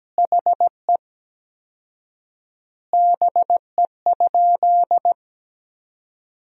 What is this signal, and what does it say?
Morse code, heard as a keyed tone, about 17 wpm, 700 Hz. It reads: HE BE?
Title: 4x Word-Spacing - Sentences from Top 200 Words - 17wpm